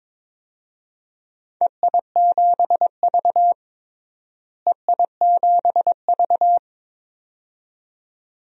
Morse code, heard as a keyed tone, about 22 wpm, 700 Hz. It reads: EI7V EI7V
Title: DXCC-100 Call Signs - 4 Characters - Repeated 2x - 22wpm